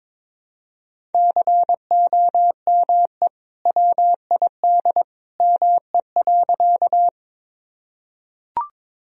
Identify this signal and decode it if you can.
Morse code, heard as a keyed tone, about 22 wpm, 700 Hz. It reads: COME WID ME.
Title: Sentences from Top 200 Words and Abbreviations 22wpm